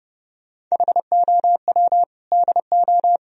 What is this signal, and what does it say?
Morse code, heard as a keyed tone, about 40 wpm, 700 Hz. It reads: HOW DO